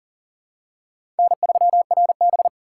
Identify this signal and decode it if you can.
Morse code, heard as a keyed tone, about 40 wpm, 690 Hz. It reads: N3RB
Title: USA Call Signs - 4 Characters - 2x Character Spacing - 40wpm